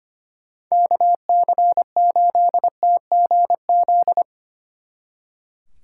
Morse code, heard as a keyed tone, about 32 wpm, 700 Hz. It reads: KC8TGZ